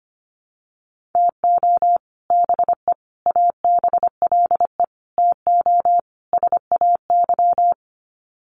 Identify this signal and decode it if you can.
Morse code, heard as a keyed tone, about 25 wpm, 700 Hz. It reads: TO BE ABLE TO SAY
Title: Sets of 5 Words Encore2 25wpm